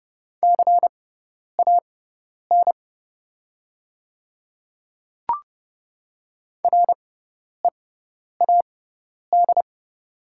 Morse code, heard as a keyed tone, about 30 wpm, spaced as for 10 wpm, 690 Hz. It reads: CAN READ
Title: 6x Character Spacing - Top 300 Words - 30wpm